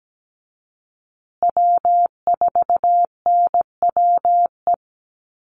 Morse code, heard as a keyed tone, about 17 wpm, 700 Hz. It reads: W4NWE
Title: USA Call Signs Repeated 2x 17wpm